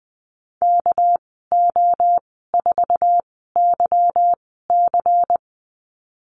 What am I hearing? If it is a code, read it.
Morse code, heard as a keyed tone, about 20 wpm, spaced as for 14 wpm, 700 Hz. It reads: KO4YC
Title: 207 Lesson 33g - Letter j - Call Signs 20wpm